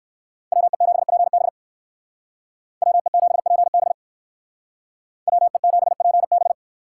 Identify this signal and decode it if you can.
Morse code, heard as a keyed tone, about 54 wpm, 700 Hz. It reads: WE7PB WE7PB WE7PB
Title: Speed Racing - USA 2x2 Call Signs 45wpm